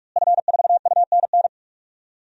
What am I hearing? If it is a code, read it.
Morse code, heard as a keyed tone, about 46 wpm, 700 Hz. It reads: F4UNN
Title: DXCC-100 Call Signs - 5 Characters - Repeated 2x - 45wpm